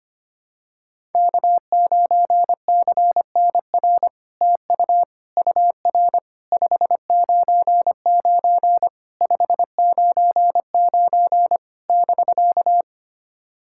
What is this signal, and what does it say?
Morse code, heard as a keyed tone, about 25 wpm, 700 Hz. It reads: K9CNR TU UR 599 599 <BK>
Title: SOTA QSO Practice 25wpm